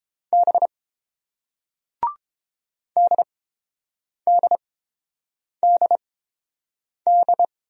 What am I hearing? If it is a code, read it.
Morse code, heard as a keyed tone, about 30 wpm, spaced as for 7 wpm, 700 Hz. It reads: B DDDD